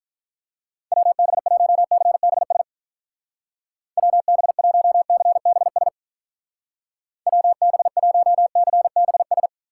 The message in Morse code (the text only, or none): WB1CBS WB1CBS WB1CBS